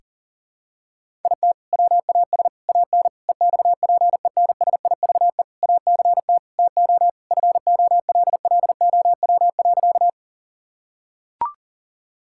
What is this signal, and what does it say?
Morse code, heard as a keyed tone, about 40 wpm, 700 Hz. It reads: IT WAS AN EXPENSIVE ACT TO FOLLOW.
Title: Sentences from Top 1000 Words - Review Part 6 - 40wpm